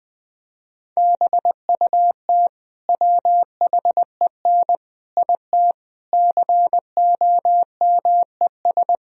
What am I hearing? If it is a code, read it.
Morse code, heard as a keyed tone, about 20 wpm, 700 Hz. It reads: BUT WHEN IT COMES
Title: Speed-Racing - Sets of 5 Words 20wpm